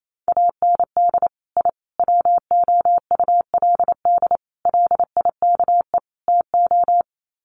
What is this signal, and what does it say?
Morse code, heard as a keyed tone, about 28 wpm, 700 Hz. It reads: AND I WOULD LIKE TO
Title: Sets of 5 Words Encore 28wpm